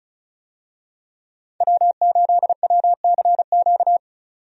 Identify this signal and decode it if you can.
Morse code, heard as a keyed tone, about 34 wpm, 700 Hz. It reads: W8WCQ
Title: USA Call Signs - 5 Characters - 35wpm